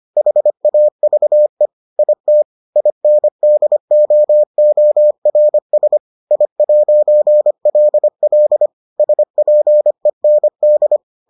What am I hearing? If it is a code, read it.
Morse code, heard as a keyed tone, about 25 wpm, 600 Hz. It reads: HAVE IT INDOORS I'LL SPEND